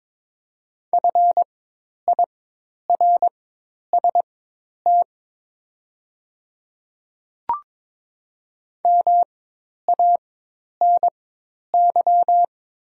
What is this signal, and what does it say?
Morse code, heard as a keyed tone, about 22 wpm, spaced as for 10 wpm, 700 Hz. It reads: FIRST MANY
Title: 4x Character Spacing - Top 200 Words - 22wpm